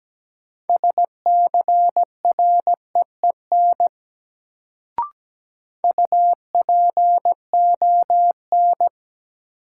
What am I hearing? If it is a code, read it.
Morse code, heard as a keyed tone, about 17 wpm, 700 Hz. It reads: SCREEN UPON
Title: Sentences from Top 1000 Words - Review Part 6 - New Word Drill - 17wpm